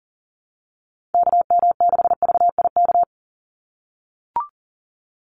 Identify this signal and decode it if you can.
Morse code, heard as a keyed tone, about 40 wpm, 700 Hz. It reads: KM6VIK
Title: USA Call Signs 40wpm